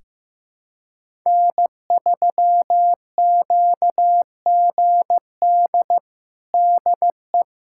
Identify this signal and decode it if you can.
Morse code, heard as a keyed tone, about 15 wpm, 700 Hz. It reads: N3QGD DE
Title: USA Callsign DE Callsign - 5 Characters - 15wpm